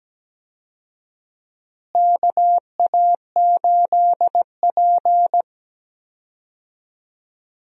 Morse code, heard as a keyed tone, about 17 wpm, 700 Hz. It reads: KA8P
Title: USA Call Signs - 4 Characters - Repeated 2x - 17wpm